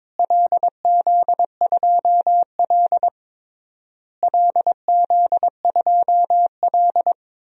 Morse code, heard as a keyed tone, about 22 wpm, 700 Hz. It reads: LZ2L LZ2L
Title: DXCC-100 Call Signs - 4 Characters - Repeated 3x - 22wpm